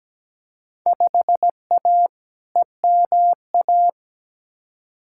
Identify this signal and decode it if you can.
Morse code, heard as a keyed tone, about 17 wpm, 700 Hz. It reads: 5A EMA
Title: Field Day Exchanges Encore - 17wpm